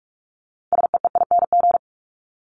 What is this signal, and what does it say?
Morse code, heard as a keyed tone, about 44 wpm, 700 Hz. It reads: SEEING
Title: ING Suffix - Top 100 Words with Base Words 45wpm